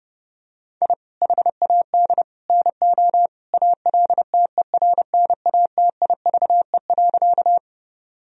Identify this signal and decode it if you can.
Morse code, heard as a keyed tone, about 28 wpm, 700 Hz. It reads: I HAD NO ALTERNATIVE.
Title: Sentences from Top 1000 Words - Review Part 4 - 30wpm